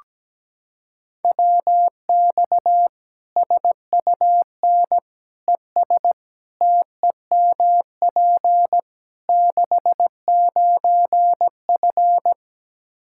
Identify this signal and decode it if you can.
Morse code, heard as a keyed tone, about 17 wpm, 700 Hz. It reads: WX SUN ES TEMP 69F
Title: K1USN SST QSO Practice 17wpm